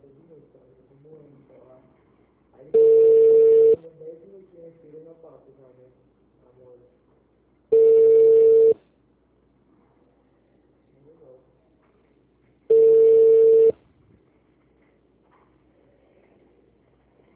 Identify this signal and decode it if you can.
Ringback tone (China, 450 Hz), with speech.